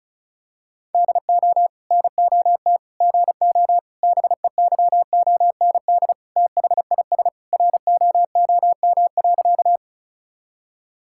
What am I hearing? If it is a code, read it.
Morse code, heard as a keyed tone, about 34 wpm, 700 Hz. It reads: DO NOT GO BEYOND THIS ROOM.
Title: Sentences from Top 1000 Words - Review Part 6 - 35wpm